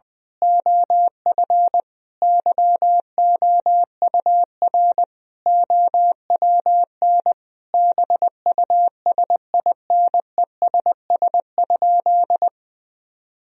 Morse code, heard as a keyed tone, about 21 wpm, 700 Hz. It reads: OF YOUR OWN BUSINESS?
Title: Sentences from Top 1000 Words - Review Part 4 - 20wpm